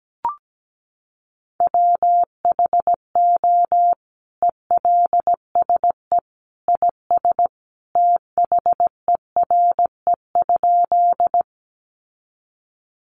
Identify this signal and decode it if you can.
Morse code, heard as a keyed tone, about 17 wpm, 700 Hz. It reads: WHO ELSE IS THERE?
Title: Sentences from Top 800 Words 17wpm